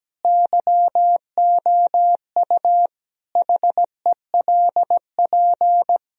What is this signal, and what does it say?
Morse code, heard as a keyed tone, about 17 wpm, 700 Hz. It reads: YOU HELP
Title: Sentences from Top 1000 Words - Review Part 3 - 17wpm